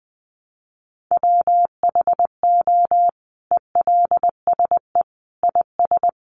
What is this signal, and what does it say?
Morse code, heard as a keyed tone, about 20 wpm, 700 Hz. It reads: WHO ELSE IS
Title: Sentences from Top 800 Words 20wpm